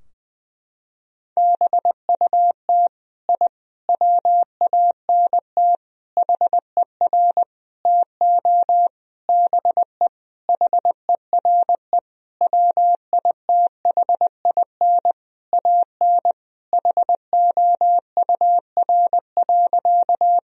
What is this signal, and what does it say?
Morse code, heard as a keyed tone, about 21 wpm, 700 Hz. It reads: BUT I WANT HER TO BE HERE WITHIN AN HOUR.
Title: Sentences from Top 1000 Words - Review Part 4 - 20wpm